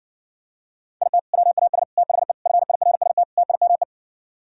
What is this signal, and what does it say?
Morse code, heard as a keyed tone, about 59 wpm, 700 Hz. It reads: IT WAS THE FIRST TIME